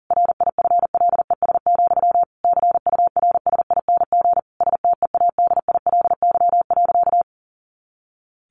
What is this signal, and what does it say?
Morse code, heard as a keyed tone, about 40 wpm, 700 Hz. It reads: RIFLES, CURSING STEADILY.
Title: A Sound Of Thunder 40wpm